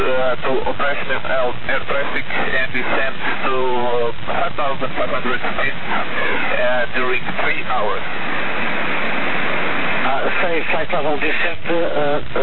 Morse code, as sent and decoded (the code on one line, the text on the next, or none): none